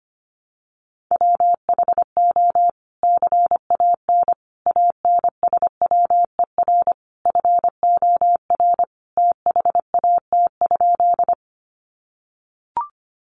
Morse code, heard as a keyed tone, about 25 wpm, 700 Hz. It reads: WHO CAN ANSWER FOR THAT?
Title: Sentences from Top 300 Words 25wpm